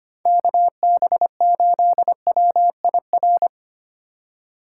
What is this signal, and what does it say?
Morse code, heard as a keyed tone, about 33 wpm, 700 Hz. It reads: KB8WIR